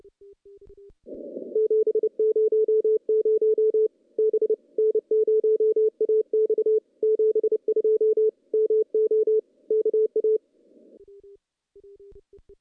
Morse code, heard as a keyed tone, about 29 wpm, 440 Hz. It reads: 700 B N0AX 72 MO KA
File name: contests / CW-SS-k5tr- -0600